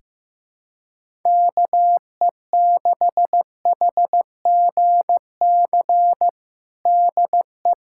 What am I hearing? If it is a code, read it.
Morse code, heard as a keyed tone, about 16 wpm, 700 Hz. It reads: KE6HGC DE
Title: USA Callsign DE Callsign - 6 Characters - 15wpm